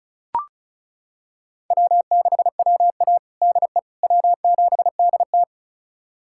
Morse code, heard as a keyed tone, about 36 wpm, 700 Hz. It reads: W6WA DE W7DT
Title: USA Callsign DE Callsign - 4 Characters - 35wpm